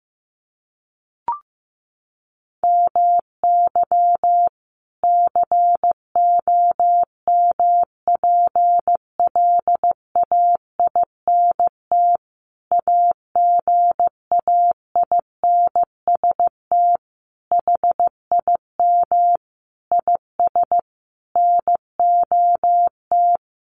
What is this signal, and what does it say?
Morse code, heard as a keyed tone, about 15 wpm, 700 Hz. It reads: MY COMPLAINT AGAINST HIM IS NOT
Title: Sentences from Top 1400 Words 15wpm